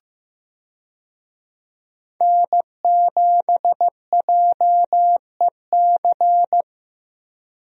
Morse code, heard as a keyed tone, about 16 wpm, 700 Hz. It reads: N7JEC
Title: USA Call Signs - 5 Characters - 15wpm